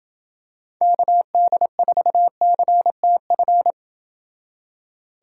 Morse code, heard as a keyed tone, about 27 wpm, 700 Hz. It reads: KD4CTF